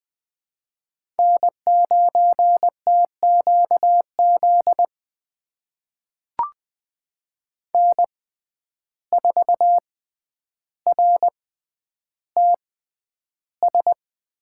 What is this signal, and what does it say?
Morse code, heard as a keyed tone, about 20 wpm, spaced as for 10 wpm, 700 Hz. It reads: N9TQZ N4RTS
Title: USA Call Signs - 5 Characters - 6x Character Spacing - 20wpm